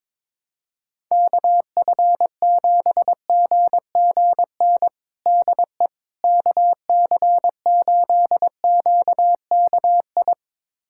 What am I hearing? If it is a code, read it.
Morse code, heard as a keyed tone, about 21 wpm, 700 Hz. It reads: KF7GGN DE KC8QKI
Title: USA Callsign DE Callsign 22wpm